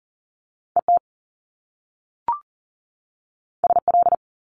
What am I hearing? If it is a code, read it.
Morse code, heard as a keyed tone, about 39 wpm, 700 Hz. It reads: ET SL
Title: Top Two Letter Combinations in Words 40wpm